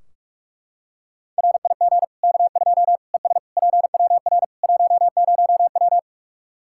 Morse code, heard as a keyed tone, about 42 wpm, 700 Hz. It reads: RIG K2 ES PWR 10W
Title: Practice QSO - Exchange 2 - Info - 45wpm